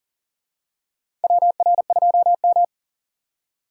Morse code, heard as a keyed tone, about 40 wpm, 700 Hz. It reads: WR2M
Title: USA Call Signs - 4x Character Spacing - 40wpm